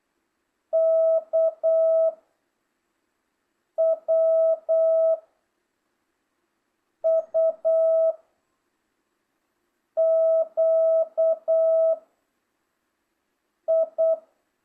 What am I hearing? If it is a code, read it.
Morse code, heard as a keyed tone, about 8 wpm, 1300 Hz, with noise. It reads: K W U Q I